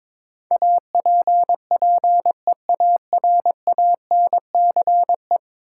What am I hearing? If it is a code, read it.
Morse code, heard as a keyed tone, about 22 wpm, 700 Hz. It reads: APPEARANCE
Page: morse-code-ninja-practice-22wpm